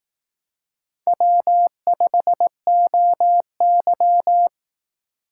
Morse code, heard as a keyed tone, about 18 wpm, 700 Hz. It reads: W5OY
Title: Speed Racing - USA 1x2 Call Signs 15wpm